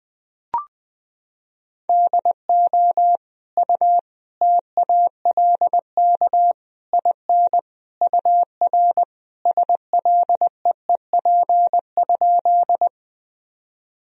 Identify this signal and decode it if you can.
Morse code, heard as a keyed tone, about 20 wpm, 700 Hz. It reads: DO U TALK IN UR SLEEP?
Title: Sentences from Top 500 Words and Abbreviations 20wpm